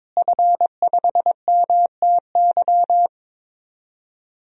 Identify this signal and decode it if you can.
Morse code, heard as a keyed tone, about 22 wpm, 700 Hz. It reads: F5MTY
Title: DXCC-100 Call Signs - 5 Characters - Repeated 3x - 22wpm